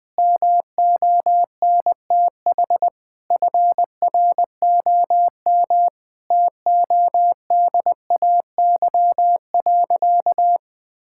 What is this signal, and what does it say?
Morse code, heard as a keyed tone, about 20 wpm, 700 Hz. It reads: MONTH FROM TODAY.
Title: Sentences from Top 1000 Words - Review Part 5 - 20wpm